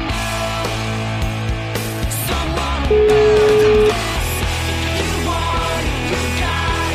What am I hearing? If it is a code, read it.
Ringback tone (Europe, 425 Hz), with noise.